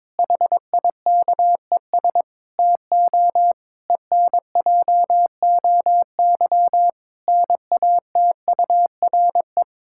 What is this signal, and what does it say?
Morse code, heard as a keyed tone, about 22 wpm, 700 Hz. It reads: HIKES TO ENJOY NATURE
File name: Difficult Sentences with Hints - 22wpm